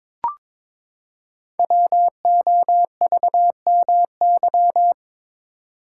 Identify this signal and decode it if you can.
Morse code, heard as a keyed tone, about 22 wpm, 700 Hz. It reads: WOVMY
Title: Code Groups - 5 Letters - 22wpm